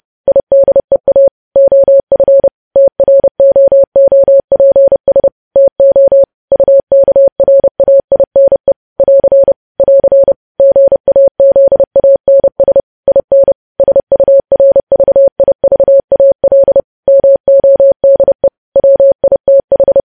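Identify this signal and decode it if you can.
Morse code, heard as a keyed tone, about 30 wpm, 550 Hz. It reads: IDEA OF TROOPS TO UKRAINE <AR> <AR> GAZANS IN SURVIVAL MODE WITH
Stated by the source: Morse code news headlines sent at 30, preceded by vvv vvv.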